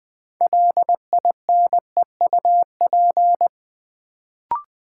Morse code, heard as a keyed tone, about 20 wpm, 700 Hz. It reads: LINEUP
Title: Build A Word - 3 Letters At A Time - 6-Letter Words - 20wpm